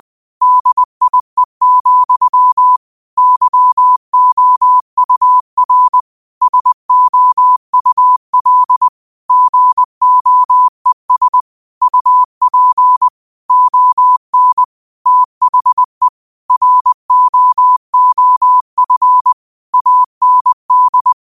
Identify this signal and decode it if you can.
Morse code, heard as a keyed tone, about 20 wpm, 1000 Hz. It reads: DIE, YOUR SOUL GOES UP ON THE ROOF AND